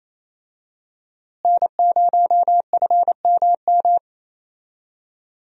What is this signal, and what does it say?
Morse code, heard as a keyed tone, about 28 wpm, 700 Hz. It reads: N0FMM
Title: USA Call Signs - 5 Characters - Repeated 3x - 28wpm